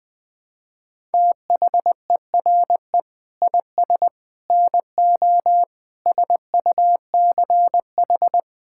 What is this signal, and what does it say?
Morse code, heard as a keyed tone, about 20 wpm, 700 Hz. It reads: THERE IS NO SUCH
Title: Sets of 6 Words - 20wpm